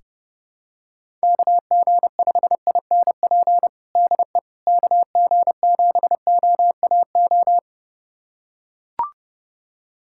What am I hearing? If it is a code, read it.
Morse code, heard as a keyed tone, about 32 wpm, 700 Hz. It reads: KG5INP DE KG7OAO
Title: USA Callsign DE Callsign - 6 Characters - 30wpm